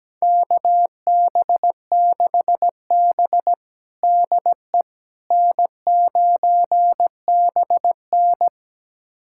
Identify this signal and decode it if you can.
Morse code, heard as a keyed tone, about 17 wpm, 700 Hz. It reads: KB6B DE N9BN
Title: USA Callsign DE Callsign - 4 Characters - 17wpm